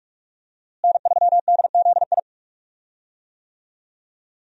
Morse code, heard as a keyed tone, about 42 wpm, 700 Hz. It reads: N3DZI
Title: USA Call Signs - 5 Characters - Repeated 2x - 45wpm